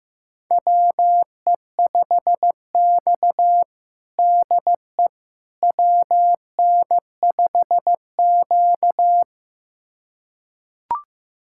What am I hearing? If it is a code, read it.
Morse code, heard as a keyed tone, about 15 wpm, 700 Hz. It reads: WE5X DE WN5Q
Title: USA Callsign DE Callsign - 4 Characters - 15wpm